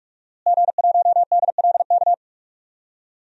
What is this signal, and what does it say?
Morse code, heard as a keyed tone, about 46 wpm, 700 Hz. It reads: G1DLK
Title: DXCC-100 Call Signs - 5 Characters - Repeated 2x - 45wpm